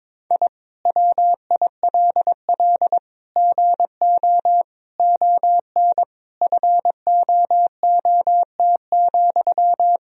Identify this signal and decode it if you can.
Morse code, heard as a keyed tone, about 22 wpm, 700 Hz. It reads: I WILL GO ON FOOT,
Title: Sentences from Top 1000 Words - Review Part 3 - 22wpm